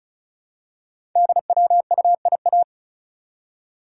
Morse code, heard as a keyed tone, about 36 wpm, 700 Hz. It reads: DWUIA
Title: Code Groups - 5 Letter:Numbers - 35wpm